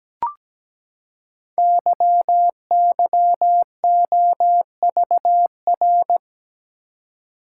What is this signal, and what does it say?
Morse code, heard as a keyed tone, about 17 wpm, 700 Hz. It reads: YYOVR
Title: Code Groups - 5 Letters - 17wpm